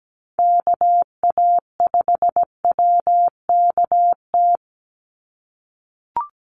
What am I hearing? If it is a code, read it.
Morse code, heard as a keyed tone, about 17 wpm, 700 Hz. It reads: KA5WKT
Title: USA Call Signs 17wpm